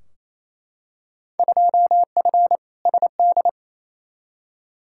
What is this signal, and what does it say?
Morse code, heard as a keyed tone, about 28 wpm, 700 Hz. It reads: 2F SD
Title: Field Day Exchanges Encore - 28wpm